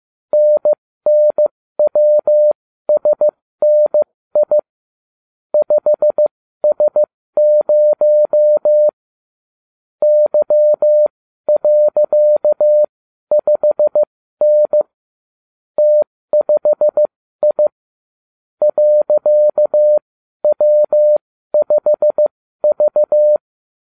Morse code, heard as a keyed tone, about 15 wpm, spaced as for 12 wpm, 600 Hz. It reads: NNWSNI 5S0 Y.5N T5I .W5V